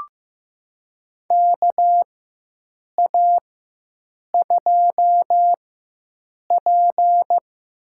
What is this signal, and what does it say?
Morse code, heard as a keyed tone, about 15 wpm, spaced as for 7 wpm, 700 Hz. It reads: KA2P
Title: USA Call Signs - 4x Character Spacing - 15wpm